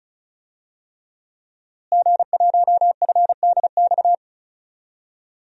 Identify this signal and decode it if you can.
Morse code, heard as a keyed tone, about 36 wpm, 700 Hz. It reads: G1FDX
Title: DXCC-100 Call Signs - 5 Characters - Repeated 3x - 35wpm